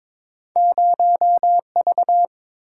Morse code, heard as a keyed tone, about 22 wpm, 700 Hz. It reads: 0V
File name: practice / 22wpm / Two Letter-Number Combinations - 22wpm